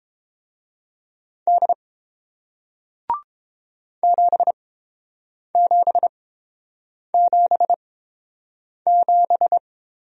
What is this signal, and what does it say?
Morse code, heard as a keyed tone, about 30 wpm, spaced as for 7 wpm, 700 Hz. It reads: D 7777